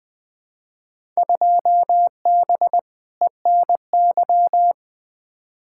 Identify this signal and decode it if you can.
Morse code, heard as a keyed tone, about 20 wpm, 700 Hz. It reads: 2B ENY
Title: Field Day Exchanges Encore - 20wpm